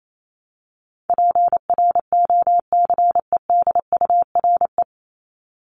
Morse code, heard as a keyed tone, about 28 wpm, 700 Hz. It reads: PROCEDURE
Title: Top 1751-2000 Words 28wpm